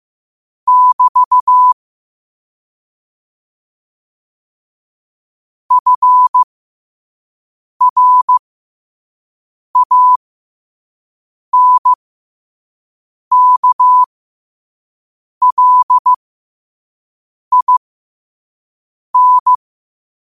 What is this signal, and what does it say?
Morse code, heard as a keyed tone, about 15 wpm, spaced as for 5 wpm, 1000 Hz. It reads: <BT> FRANKLIN
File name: 5wpm